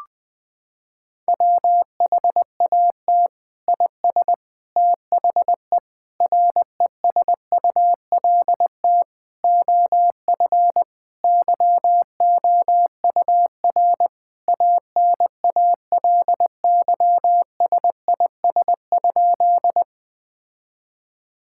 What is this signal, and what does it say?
Morse code, heard as a keyed tone, about 20 wpm, 700 Hz. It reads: WHAT IS THE RESULT OF YOUR ANALYSIS?
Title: Sentences from Top 1000 Words - Review Part 5 - 20wpm